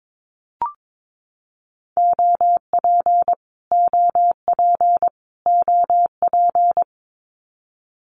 Morse code, heard as a keyed tone, about 22 wpm, 700 Hz. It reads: OP OP OP
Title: Abbreviations Repeated 3x - 22wpm